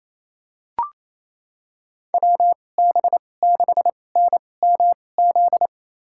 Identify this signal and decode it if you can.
Morse code, heard as a keyed tone, about 28 wpm, spaced as for 20 wpm, 690 Hz. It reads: WB6NMZ
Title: USA Call Signs - 6 Characters - 2x Character Spacing - 28wpm